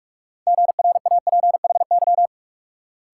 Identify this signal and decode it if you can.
Morse code, heard as a keyed tone, about 56 wpm, 700 Hz. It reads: GRAPHY
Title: Speed-Racing Multi-letter Phonemes - 45wpm